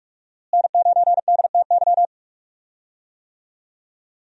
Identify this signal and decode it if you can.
Morse code, heard as a keyed tone, about 42 wpm, 700 Hz. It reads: N9DTY
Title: USA Call Signs - 5 Characters - Repeated 3x - 45wpm